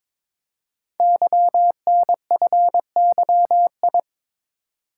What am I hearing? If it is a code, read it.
Morse code, heard as a keyed tone, about 22 wpm, 700 Hz. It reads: YNFYI
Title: Code Groups - 5 Letters - 22wpm